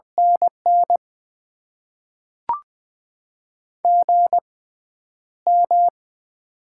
Morse code, heard as a keyed tone, about 20 wpm, spaced as for 10 wpm, 690 Hz. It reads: NN GM
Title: Top 20 QSO Elements - 6x Character Spacing - 20wpm